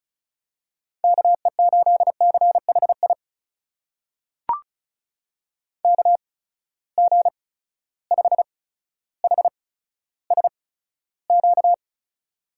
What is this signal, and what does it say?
Morse code, heard as a keyed tone, about 36 wpm, spaced as for 15 wpm, 700 Hz. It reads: KE8CHI KG5HSQ
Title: USA Call Signs - 6 Characters - 8x Character Spacing - 35wpm